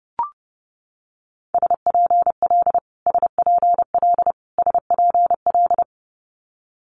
Morse code, heard as a keyed tone, about 30 wpm, 700 Hz. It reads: SPL SPL SPL
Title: Multi-letter Phonemes Repeated 3x 30wpm